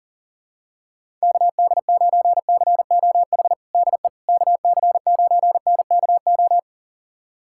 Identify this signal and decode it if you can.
Morse code, heard as a keyed tone, about 40 wpm, 700 Hz. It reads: KD9COH DE KC9NKO
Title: USA Callsign DE Callsign - 6 Characters - 40wpm